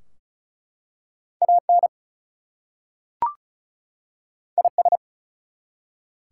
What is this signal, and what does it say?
Morse code, heard as a keyed tone, about 33 wpm, 700 Hz. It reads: AN IS
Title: Two Letter Words - 35wpm